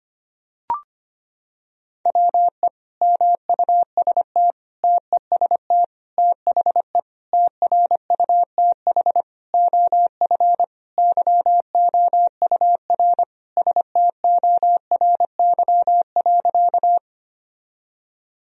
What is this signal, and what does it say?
Morse code, heard as a keyed tone, about 25 wpm, 700 Hz. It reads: WE MUST TEST THE TRUTH OF YOUR STORY.
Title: Sentences from Top 1000 Words - Review Part 3 - 25wpm